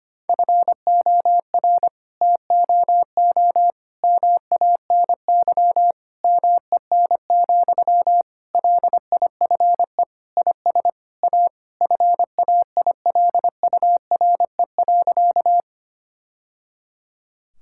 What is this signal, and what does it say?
Morse code, heard as a keyed tone, about 25 wpm, 700 Hz. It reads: FOR TOO MANY MEN, LIFE IS A FAILURE.
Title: Sentences from Top 1000 Words - Review Part 6 - 25wpm